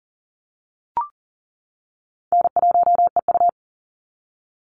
Morse code, heard as a keyed tone, about 40 wpm, 700 Hz. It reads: N1EU
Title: NA SOTA Chaser Callsigns 40wpm